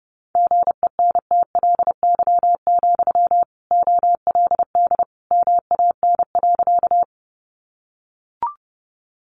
Morse code, heard as a keyed tone, about 30 wpm, 700 Hz. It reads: GENTLY, OLD MAN.
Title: Sentences from Top 1500 Words 30wpm